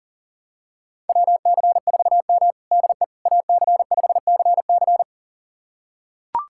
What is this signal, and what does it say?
Morse code, heard as a keyed tone, about 37 wpm, 700 Hz. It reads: WC4M DE AC5CC
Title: USA Callsign DE Callsign 40wpm